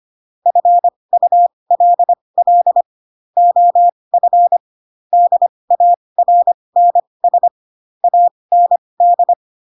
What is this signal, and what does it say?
Morse code, heard as a keyed tone, about 25 wpm, spaced as for 20 wpm, 700 Hz. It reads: FULL OF DARNS AND
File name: Pinocchio / Pinocchio ch8 15 25 ik1hge